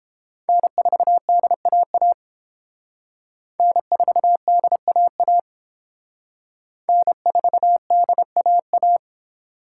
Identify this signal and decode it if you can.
Morse code, heard as a keyed tone, about 31 wpm, 700 Hz. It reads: N4DAA N4DAA N4DAA